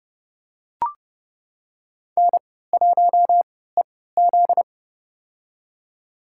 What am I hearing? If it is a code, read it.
Morse code, heard as a keyed tone, about 30 wpm, spaced as for 17 wpm, 700 Hz. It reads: N1EZ
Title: USA Call Signs - 4 Characters - 3x Character Spacing - 30wpm